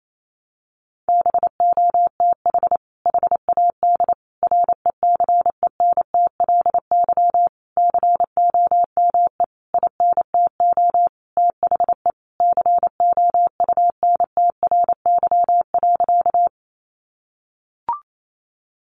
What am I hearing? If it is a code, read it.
Morse code, heard as a keyed tone, about 28 wpm, 700 Hz. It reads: BOTH HAD RECENTLY COME INTO THE COUNTRY.
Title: Sentences from Top 1200 Words 28wpm